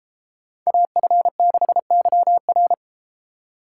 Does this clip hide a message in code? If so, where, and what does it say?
Yes: Morse code from 0.7 to 2.7 s, heard as a keyed tone, about 34 wpm, 700 Hz. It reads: AF6YR